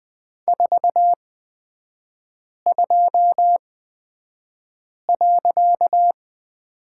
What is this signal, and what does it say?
Morse code, heard as a keyed tone, about 20 wpm, spaced as for 5 wpm, 700 Hz. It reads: 42.
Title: 255 Lesson 41e - Period - Three Characters 20wpm